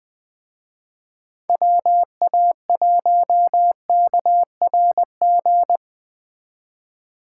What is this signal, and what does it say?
Morse code, heard as a keyed tone, about 20 wpm, 700 Hz. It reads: WA1KRG